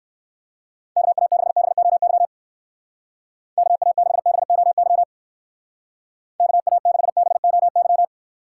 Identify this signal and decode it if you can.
Morse code, heard as a keyed tone, about 54 wpm, 700 Hz. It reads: KA6BOY KA6BOY KA6BOY
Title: Speed Racing - USA Call Signs 45wpm